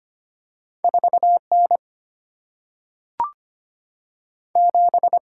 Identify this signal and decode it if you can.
Morse code, heard as a keyed tone, about 25 wpm, 700 Hz. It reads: 4 N 7
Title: Single-Letter Single-Number 25wpm